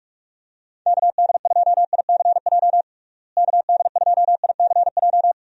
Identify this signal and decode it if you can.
Morse code, heard as a keyed tone, about 41 wpm, 700 Hz. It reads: KD2ICJ KD2ICJ
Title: USA Call Signs - 6 Characters - Repeated 3x - 45wpm